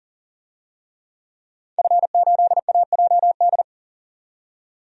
Morse code, heard as a keyed tone, about 40 wpm, 700 Hz. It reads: F8AJD
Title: DXCC-100 Call Signs - 5 Characters - 40wpm